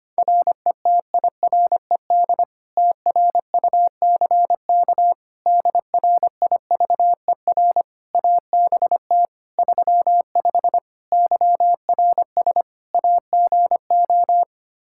Morse code, heard as a keyed tone, about 25 wpm, 700 Hz. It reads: RETIRED TRUCK DRIVER ABT 35 YRS AGO
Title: Practice QSO - Exchange 3 - More Info - 25wpm